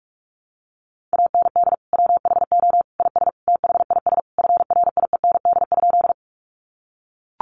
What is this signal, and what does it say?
Morse code, heard as a keyed tone, about 44 wpm, 700 Hz. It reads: AND WHO IS THIS FRIEND?
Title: Sentences from Top 300 Words 45wpm